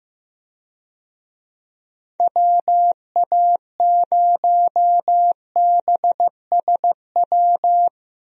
Morse code, heard as a keyed tone, about 15 wpm, 700 Hz. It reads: WA0BSW
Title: USA Call Signs - 6 Characters - 15wpm